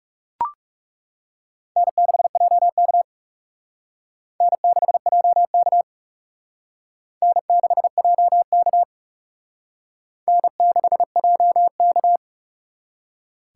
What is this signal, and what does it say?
Morse code, heard as a keyed tone, about 37 wpm, 700 Hz. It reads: N6JK N6JK N6JK N6JK
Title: Speed Racing - USA 1x2 Call Signs 30wpm